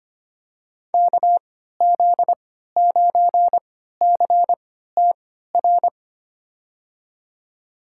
Morse code, heard as a keyed tone, about 25 wpm, spaced as for 14 wpm, 700 Hz. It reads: KZ9CTR
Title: USA Call Signs - 6 Characters - 3x Character Spacing - 25wpm